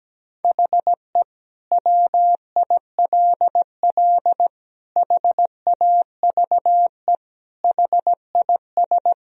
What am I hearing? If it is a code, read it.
Morse code, heard as a keyed tone, about 17 wpm, 700 Hz. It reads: HE WILL HAVE HIS
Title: Sentences from Top 1000 Words - Review Part 3 - 17wpm